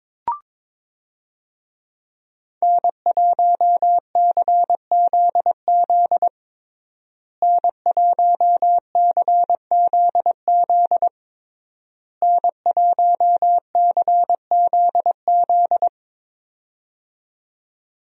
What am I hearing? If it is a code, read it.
Morse code, heard as a keyed tone, about 21 wpm, 700 Hz. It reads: N1CZZ N1CZZ N1CZZ
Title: USA Call Signs - 5 Characters - Repeated 3x - 22wpm